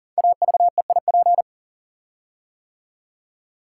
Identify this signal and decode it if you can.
Morse code, heard as a keyed tone, about 39 wpm, 700 Hz. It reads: AVEIP
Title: Code Groups - 5 Letters - 40wpm